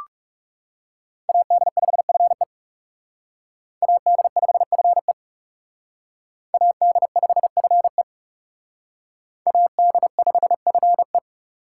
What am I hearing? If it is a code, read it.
Morse code, heard as a keyed tone, about 37 wpm, 700 Hz. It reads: AD5FE AD5FE AD5FE AD5FE
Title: Speed Racing - USA 2x2 Call Signs 30wpm